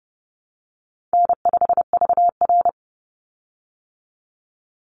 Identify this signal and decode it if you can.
Morse code, heard as a keyed tone, about 30 wpm, 700 Hz. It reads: N5VR
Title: Easy Call Signs Repeated 2x - 30wpm